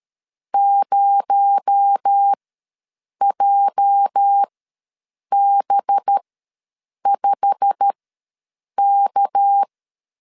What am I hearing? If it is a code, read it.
Morse code, heard as a keyed tone, about 13 wpm, spaced as for 7 wpm, 800 Hz. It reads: 0JB5K
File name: LEZIONI / 8 YJ45 / 8 YJ45 ws 12wpm 3146